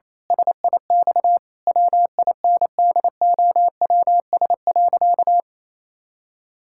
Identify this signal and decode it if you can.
Morse code, heard as a keyed tone, about 28 wpm, 700 Hz. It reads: SIX WINDOWS.
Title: Pangram Challenge v2 - 28wpm